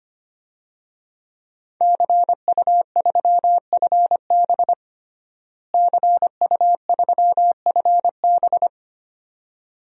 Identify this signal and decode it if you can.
Morse code, heard as a keyed tone, about 25 wpm, 700 Hz. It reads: CU3FB CU3FB
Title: DXCC-100 Call Signs - 5 Characters - Repeated 2x - 25wpm